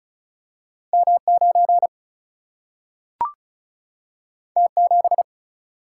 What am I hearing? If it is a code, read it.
Morse code, heard as a keyed tone, about 36 wpm, 700 Hz. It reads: M 9 T 7
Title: Single-Letter Single-Number 35wpm